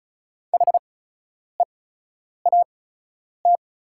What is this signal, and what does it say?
Morse code, heard as a keyed tone, about 36 wpm, spaced as for 10 wpm, 700 Hz. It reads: HEAT
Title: 8x Character Spacing - Top 500 Words - 35wpm